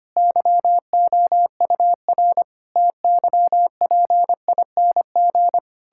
Morse code, heard as a keyed tone, about 25 wpm, 700 Hz. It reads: YOUR TYPING
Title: Pangram Challenge v2 - 25wpm